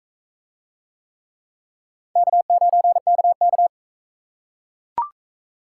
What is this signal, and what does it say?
Morse code, heard as a keyed tone, about 39 wpm, 700 Hz. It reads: K9KK